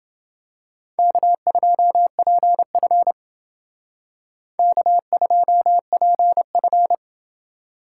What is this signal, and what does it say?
Morse code, heard as a keyed tone, about 28 wpm, 700 Hz. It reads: K2PF K2PF